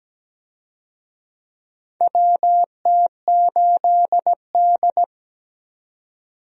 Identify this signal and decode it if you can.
Morse code, heard as a keyed tone, about 17 wpm, 700 Hz. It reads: WT8D
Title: USA Call Signs - 4 Characters - 17wpm